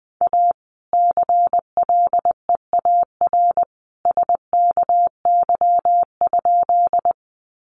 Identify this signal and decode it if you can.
Morse code, heard as a keyed tone, about 20 wpm, 700 Hz. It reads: A CLEAR SKY?
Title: Sentences from Top 1200 Words 20wpm